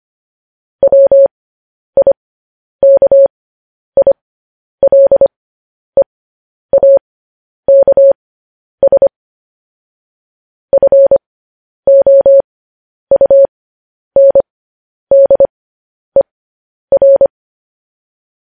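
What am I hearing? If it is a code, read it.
Morse code, heard as a keyed tone, about 25 wpm, spaced as for 10 wpm, 550 Hz. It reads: WIKILEAKS FOUNDER
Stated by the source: Morse code news headlines sent at 10.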